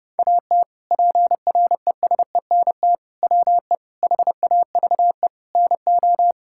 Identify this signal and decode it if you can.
Morse code, heard as a keyed tone, about 29 wpm, 700 Hz. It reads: AT PRESENT WE HAVE NO
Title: Sentences from Top 1000 Words - Review Part 5 - 30wpm